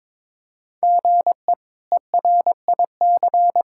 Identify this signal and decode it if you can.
Morse code, heard as a keyed tone, about 22 wpm, 700 Hz. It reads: GE ERIC
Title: K1USN SST QSO Practice - 22wpm